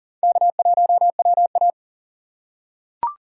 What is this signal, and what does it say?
Morse code, heard as a keyed tone, about 40 wpm, 700 Hz. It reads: K1WA
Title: USA Call Signs - 4x Character Spacing - 40wpm